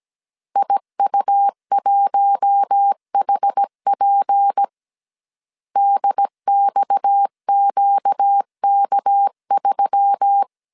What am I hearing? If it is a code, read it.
Morse code, heard as a keyed tone, about 17 wpm, 800 Hz. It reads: IU1HP DXQK3
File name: CW / DET / 1 6 1500C S A 16wpm 2635